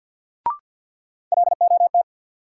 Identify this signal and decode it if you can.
Morse code, heard as a keyed tone, about 53 wpm, 700 Hz. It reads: LOT
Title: Sets of 2 Words - New Word Drill - 50wpm